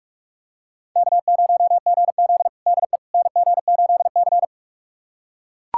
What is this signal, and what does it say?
Morse code, heard as a keyed tone, about 43 wpm, 700 Hz. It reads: K0GZ DE NG8C
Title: USA Callsign DE Callsign - 4 Characters - 45wpm